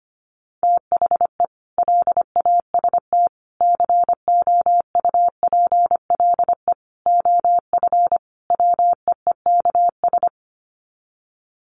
Morse code, heard as a keyed tone, about 25 wpm, 700 Hz. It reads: THE LAST COUPLE OF WEEKS
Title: Sets of 5 Words Encore2 25wpm